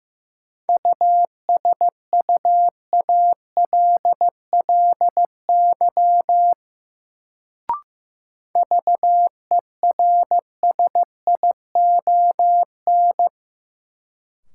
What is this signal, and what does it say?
Morse code, heard as a keyed tone, about 15 wpm, 700 Hz. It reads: USUALLY VERSION
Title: Sentences from Top 1400 Words - New Word Drill Alt - 15wpm